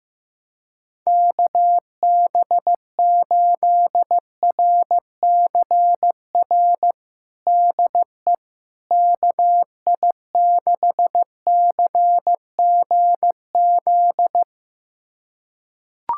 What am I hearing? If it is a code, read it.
Morse code, heard as a keyed tone, about 15 wpm, 700 Hz. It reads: KB8RCR DE KI6CGZ
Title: USA Callsign DE Callsign - 6 Characters - 15wpm